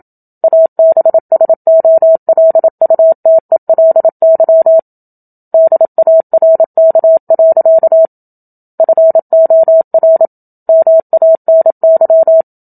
Morse code, heard as a keyed tone, about 27 wpm, 650 Hz. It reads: ABSOLUTELY DARK. FOR MANY
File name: Book Two / 01-Chapter 1 - 27 WPM